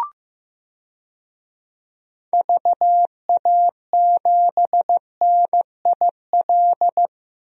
Morse code, heard as a keyed tone, about 15 wpm, 700 Hz. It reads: VA7NIL
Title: DXCC-100 Call Signs - 6 Characters - 15wpm